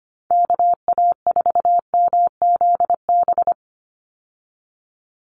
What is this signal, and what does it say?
Morse code, heard as a keyed tone, about 25 wpm, 700 Hz. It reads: KA4MZB
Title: USA Call Signs 25wpm